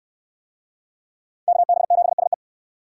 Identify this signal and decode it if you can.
Morse code, heard as a keyed tone, about 69 wpm, 700 Hz. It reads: KB7SE